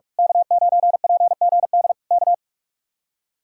Morse code, heard as a keyed tone, about 41 wpm, 700 Hz. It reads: K9PGD K
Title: Speed-Racing Common QSO Phrases - 45wpm